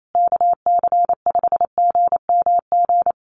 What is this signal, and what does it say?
Morse code, heard as a keyed tone, about 28 wpm, 700 Hz. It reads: KC5GMG
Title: USA Call Signs Repeated 3x 28wpm